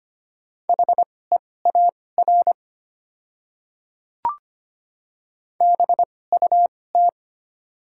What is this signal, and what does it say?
Morse code, heard as a keyed tone, about 25 wpm, spaced as for 18 wpm, 700 Hz. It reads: HEAR BUT
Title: 2x Character Spacing - Top 500 Words - 25wpm